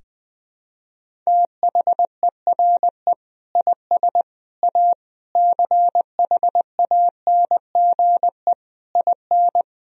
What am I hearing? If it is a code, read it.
Morse code, heard as a keyed tone, about 21 wpm, 700 Hz. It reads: THERE IS A CHANGE IN
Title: Sentences from Top 1000 Words - Review Part 1 - 20wpm